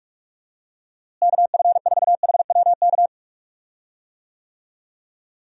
Morse code, heard as a keyed tone, about 46 wpm, 700 Hz. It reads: KF4HWK
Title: USA Call Signs - 6 Characters - Repeated 2x - 45wpm